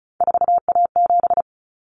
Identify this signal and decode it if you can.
Morse code, heard as a keyed tone, about 34 wpm, 700 Hz. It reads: 4A7
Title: Common Cut Numbers - 3-Digit 35wpm